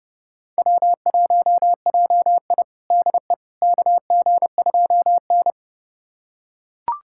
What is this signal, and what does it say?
Morse code, heard as a keyed tone, about 30 wpm, 700 Hz. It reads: W1JI DE KG2N
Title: USA Callsign DE Callsign - 4 Characters - 30wpm